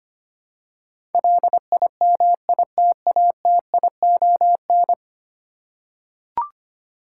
Morse code, heard as a keyed tone, about 25 wpm, 700 Hz. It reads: LIMITATION
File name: 10 Letter Words - First 3 Letters Given - 25wpm